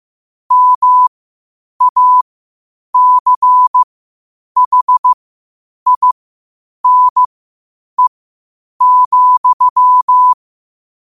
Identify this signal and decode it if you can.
Morse code, heard as a keyed tone, about 15 wpm, spaced as for 8 wpm, 1000 Hz. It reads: MACHINE,
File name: qotdfiles / 8wpm